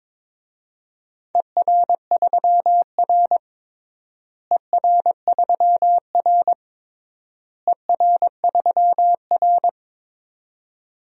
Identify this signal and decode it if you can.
Morse code, heard as a keyed tone, about 22 wpm, 700 Hz. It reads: ER3R ER3R ER3R
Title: DXCC-100 Call Signs - 4 Characters - Repeated 3x - 22wpm